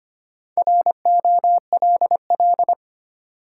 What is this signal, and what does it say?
Morse code, heard as a keyed tone, about 25 wpm, 700 Hz. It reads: ROLL
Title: Four Letter Words - 25wpm